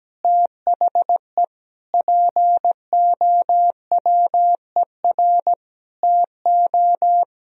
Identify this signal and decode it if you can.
Morse code, heard as a keyed tone, about 17 wpm, 700 Hz. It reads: THE POWER TO
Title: Sentences from Top 1000 Words - Review Part 3 - 17wpm